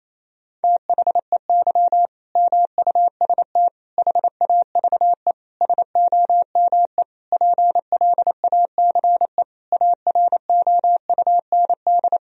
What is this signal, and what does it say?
Morse code, heard as a keyed tone, about 28 wpm, 700 Hz. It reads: THEY MUST HAVE SOME PLACE AROUND
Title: Sentences from Top 1000 Words - Review Part 3 - 28wpm